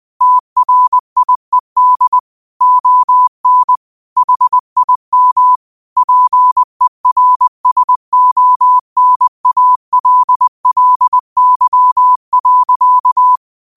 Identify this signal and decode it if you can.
Morse code, heard as a keyed tone, about 20 wpm, 1000 Hz. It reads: TRIED ON HIM PERSONALLY.
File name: qotdfiles / 20wpm